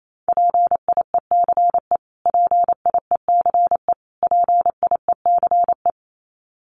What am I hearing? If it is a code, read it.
Morse code, heard as a keyed tone, about 28 wpm, 700 Hz. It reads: PIECE PIECE PIECE
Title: Top 500 Words Repeated 28wpm